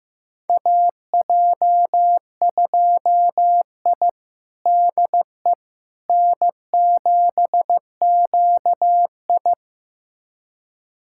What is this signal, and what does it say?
Morse code, heard as a keyed tone, about 15 wpm, 700 Hz. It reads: AJ2I DE N7QI
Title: USA Callsign DE Callsign - 4 Characters - 15wpm